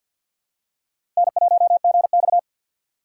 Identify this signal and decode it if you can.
Morse code, heard as a keyed tone, about 48 wpm, 700 Hz. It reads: N1GX
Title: USA Call Signs - 4 Characters - 3x Character Spacing - 50wpm